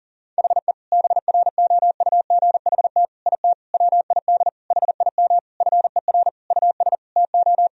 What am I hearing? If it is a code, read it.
Morse code, heard as a keyed tone, about 37 wpm, 700 Hz. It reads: HE BROUGHT IT WID HIM FER US TO
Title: Sentences from Top 500 Words and Abbreviations 40wpm